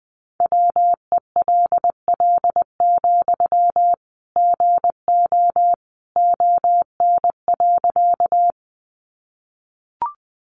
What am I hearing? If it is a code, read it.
Morse code, heard as a keyed tone, about 20 wpm, 700 Hz. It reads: WELL, GO ON.
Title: Sentences from Top 200 Words 20wpm